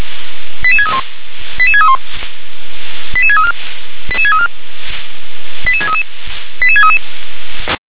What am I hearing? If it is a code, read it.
Five-tone selective call (ZVEI), heard as quick runs of five tones, tones of about 70 ms. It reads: 88631 88631 88634 88634 88633 88633
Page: SDR/ZVEI